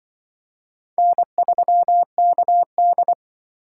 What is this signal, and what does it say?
Morse code, heard as a keyed tone, about 23 wpm, 700 Hz. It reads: N3KD
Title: Speed Racing - USA 1x2 Call Signs 20wpm